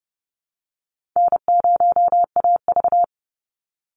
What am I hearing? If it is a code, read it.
Morse code, heard as a keyed tone, about 30 wpm, 700 Hz. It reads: N0AV
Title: Easy Call Signs - 30wpm